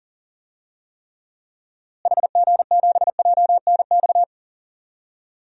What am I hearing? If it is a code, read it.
Morse code, heard as a keyed tone, about 38 wpm, 700 Hz. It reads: HG7JNX
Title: DXCC-100 Call Signs - 6 Characters - Repeated 3x - 40wpm